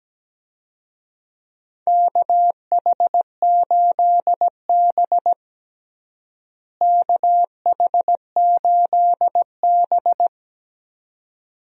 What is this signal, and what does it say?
Morse code, heard as a keyed tone, about 17 wpm, 700 Hz. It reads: KH8B KH8B
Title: USA Call Signs - 4 Characters - Repeated 2x - 17wpm